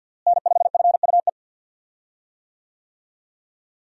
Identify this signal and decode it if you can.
Morse code, heard as a keyed tone, about 47 wpm, 700 Hz. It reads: N5FUE
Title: USA Call Signs - 5 Characters - Repeated 2x - 50wpm